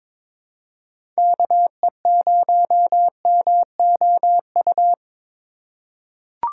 Morse code, heard as a keyed tone, about 22 wpm, 700 Hz. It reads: KE0MOU
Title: USA Call Signs - 6 Characters - 3x Character Spacing - 22wpm